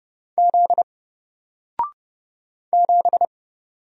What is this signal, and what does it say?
Morse code, heard as a keyed tone, about 30 wpm, 700 Hz. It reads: Z 7
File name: Single-Letter-Number-30wpm